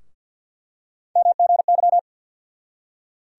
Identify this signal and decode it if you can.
Morse code, heard as a keyed tone, about 46 wpm, 700 Hz. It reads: MGY
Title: USA Call Sign Postfixes - 3 Letters 50wpm